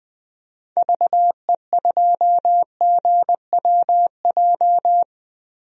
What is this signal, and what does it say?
Morse code, heard as a keyed tone, about 20 wpm, 700 Hz. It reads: VE2GWJ
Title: DXCC-100 Call Signs - 6 Characters - 20wpm